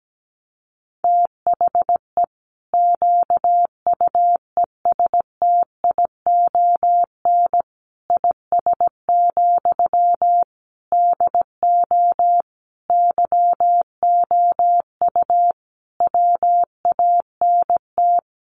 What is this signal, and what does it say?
Morse code, heard as a keyed tone, about 17 wpm, 700 Hz. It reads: THE QUESTION IS, DO YOU WANT
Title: Sentences from Top 1000 Words 17wpm